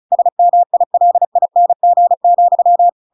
Morse code, heard as a keyed tone, about 34 wpm, 700 Hz. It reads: SMILING,